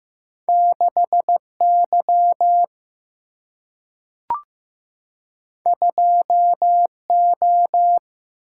Morse code, heard as a keyed tone, about 15 wpm, 700 Hz. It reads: 6 Y 2 O
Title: Single-Letter Single-Number 15wpm